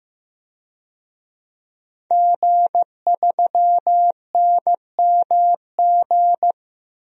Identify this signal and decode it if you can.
Morse code, heard as a keyed tone, about 15 wpm, 700 Hz. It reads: G3NMG
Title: DXCC-100 Call Signs - 5 Characters - 15wpm